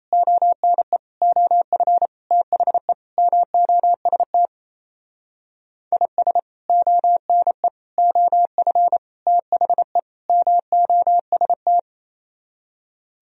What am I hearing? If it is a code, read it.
Morse code, heard as a keyed tone, about 36 wpm, 700 Hz. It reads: ONE OF THE MOST IS ONE OF THE MOST